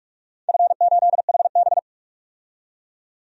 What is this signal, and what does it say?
Morse code, heard as a keyed tone, about 46 wpm, 700 Hz. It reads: F8HB
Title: DXCC-100 Call Signs - 4 Characters - 45wpm